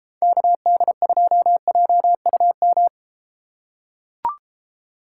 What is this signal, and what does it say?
Morse code, heard as a keyed tone, about 31 wpm, 700 Hz. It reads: KD2JUM